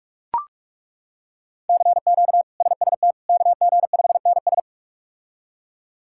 Morse code, heard as a keyed tone, about 41 wpm, 700 Hz. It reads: CQ SST KG5NS
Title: K1USN SST QSO Practice - 45wpm